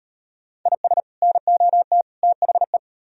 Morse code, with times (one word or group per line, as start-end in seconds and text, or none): 0.7-1.0 IS
1.2-2.0 NOT
2.2-2.8 THE